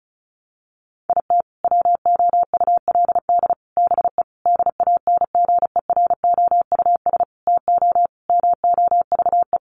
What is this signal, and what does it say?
Morse code, heard as a keyed tone, about 34 wpm, 700 Hz. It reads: IT WOULD BE DANGEROUS TO MOVE
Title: Sentences from Top 1200 Words 35wpm